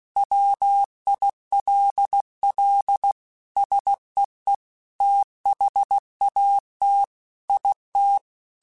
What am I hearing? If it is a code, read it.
Morse code, heard as a keyed tone, about 16 wpm, 800 Hz. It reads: WILL SEE THAT IT
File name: Technical Subjects / Technical Subjects-mp3s / OR Gate @15